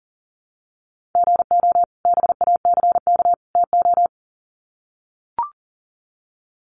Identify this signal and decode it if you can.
Morse code, heard as a keyed tone, about 40 wpm, 700 Hz. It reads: GO BACK TO
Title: Sets of 3 Words 40wpm